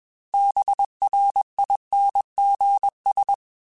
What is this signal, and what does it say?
Morse code, heard as a keyed tone, about 21 wpm, 800 Hz. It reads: BRINGS
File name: Quotes / Quotes-mp3s / Persperation @20